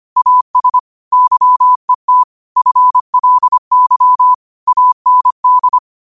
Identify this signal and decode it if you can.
Morse code, heard as a keyed tone, about 25 wpm, 1000 Hz. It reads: AS YET FLY AND
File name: qotdfiles / 25wpm